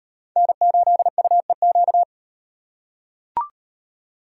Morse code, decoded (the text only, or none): N8UEQ